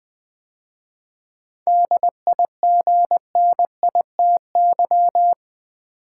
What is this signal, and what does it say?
Morse code, heard as a keyed tone, about 20 wpm, 700 Hz. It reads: DIGNITY
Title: 7 Letter Words - First 3 Letters Given - 20wpm